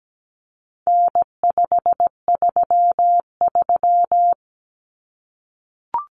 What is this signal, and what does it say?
Morse code, heard as a keyed tone, about 17 wpm, 700 Hz. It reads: N533
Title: Common Cut Numbers - 4-Digit 17wpm